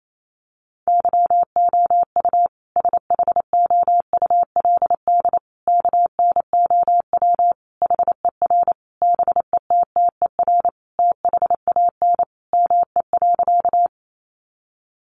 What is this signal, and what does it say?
Morse code, heard as a keyed tone, about 28 wpm, 700 Hz. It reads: YOU SHOULD KNOW HER BETTER THAN ME.
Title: Sentences from Top 400 Words 28wpm